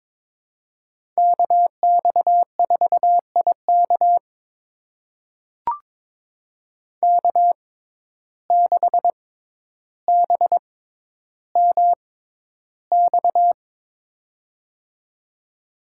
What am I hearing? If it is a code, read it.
Morse code, heard as a keyed tone, about 22 wpm, spaced as for 11 wpm, 700 Hz. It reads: KX4IK K6BMX
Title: USA Call Signs - 5 Characters - 6x Character Spacing - 22wpm